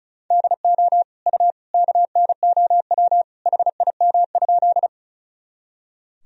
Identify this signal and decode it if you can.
Morse code, heard as a keyed tone, about 36 wpm, 700 Hz. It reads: DO U KNOW HIM?
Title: Sentences from Top 100 Words and Abbreviations 35wpm